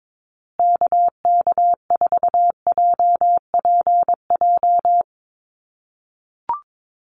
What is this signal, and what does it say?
Morse code, heard as a keyed tone, about 22 wpm, 700 Hz. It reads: KK4JPJ
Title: USA Call Signs Repeated 2x 22wpm